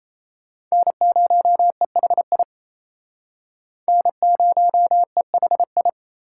Morse code, heard as a keyed tone, about 38 wpm, 700 Hz. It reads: N0EHI N0EHI